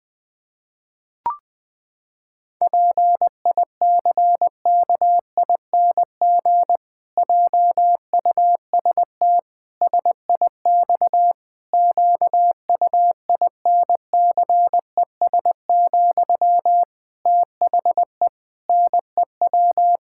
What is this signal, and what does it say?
Morse code, heard as a keyed tone, about 19 wpm, 700 Hz. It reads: PICKING JUST SIX QUINCES, THE NEW
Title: Pangram Challenge v2 - 20wpm